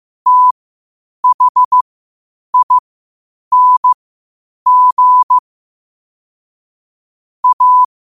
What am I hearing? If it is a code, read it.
Morse code, heard as a keyed tone, about 15 wpm, spaced as for 8 wpm, 1000 Hz. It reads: THING A
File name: qotdfiles / 8wpm